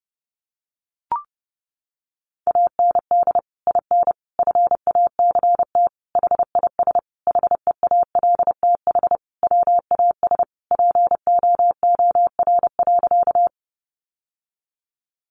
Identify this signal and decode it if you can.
Morse code, heard as a keyed tone, about 30 wpm, 700 Hz. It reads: AND IN FACT HIS HEALTH WAS POOR.
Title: Sentences from Top 1000 Words 30wpm